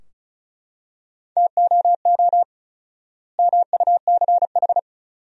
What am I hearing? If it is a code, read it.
Morse code, heard as a keyed tone, about 36 wpm, 700 Hz. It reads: TOO MUCH
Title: 4x Word-Spacing - Sentences from Top 200 Words - 35wpm